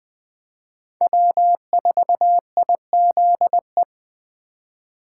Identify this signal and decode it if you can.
Morse code, heard as a keyed tone, about 20 wpm, 700 Hz. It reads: W4IZE
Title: USA Call Signs - 5 Characters - 2x Character Spacing - 20wpm